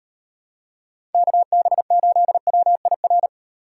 Morse code, heard as a keyed tone, about 38 wpm, 700 Hz. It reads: KB8WIR